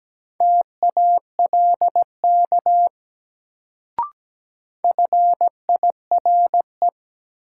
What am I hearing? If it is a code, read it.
Morse code, heard as a keyed tone, about 17 wpm, 700 Hz. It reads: TALK FIRE
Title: Sentences from Top 400 Words - New Word Drill - 17wpm